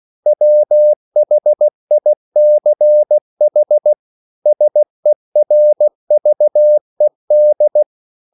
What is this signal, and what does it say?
Morse code, heard as a keyed tone, about 16 wpm, 600 Hz. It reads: WHICH SERVED
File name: CW / Practice / EnglishQuotations0116